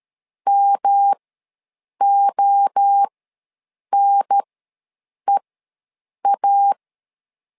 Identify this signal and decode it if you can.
Morse code, heard as a keyed tone, about 13 wpm, spaced as for 7 wpm, 800 Hz. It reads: MONEA
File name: LEZIONI / 3 MWRO / 3 MWRO ws 12wpm 1914